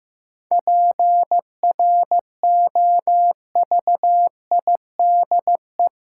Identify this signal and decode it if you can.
Morse code, heard as a keyed tone, about 15 wpm, 700 Hz. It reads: PROVIDE
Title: Sentences from Top 1000 Words - Review Part 2 - New Word Drill - 15wpm